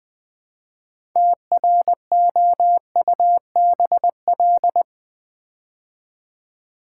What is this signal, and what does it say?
Morse code, heard as a keyed tone, about 20 wpm, 700 Hz. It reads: TROUBL
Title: Build A Word - 2 Letters At A Time - 8-Letter Words - 20wpm